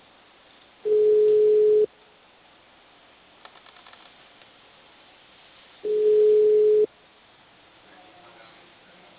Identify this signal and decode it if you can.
Ringback tone (Europe, 425 Hz).
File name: out- -2004- - - .51838